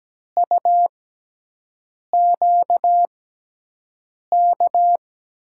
Morse code, heard as a keyed tone, about 17 wpm, spaced as for 6 wpm, 700 Hz. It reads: UQK
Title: USA Call Signs - 5 Characters - 6x Character Spacing - 17wpm